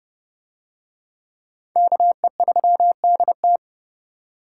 Morse code, heard as a keyed tone, about 27 wpm, 700 Hz. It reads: KE3DT